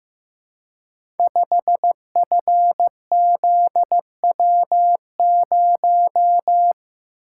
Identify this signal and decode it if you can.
Morse code, heard as a keyed tone, about 15 wpm, 700 Hz. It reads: 5FZW0
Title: Code Groups - 5 Letter:Numbers - 15wpm